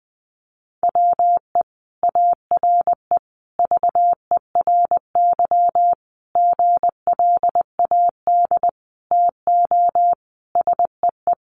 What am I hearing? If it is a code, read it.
Morse code, heard as a keyed tone, about 20 wpm, 700 Hz. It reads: WE ARE VERY GLAD TO SEE
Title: Sentences from Top 800 Words 20wpm